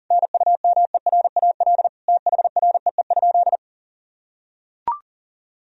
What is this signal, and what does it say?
Morse code, heard as a keyed tone, about 40 wpm, 700 Hz. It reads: NUMERAL THREE?
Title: Sentences from Top 1000 Words - Review Part 2 - 40wpm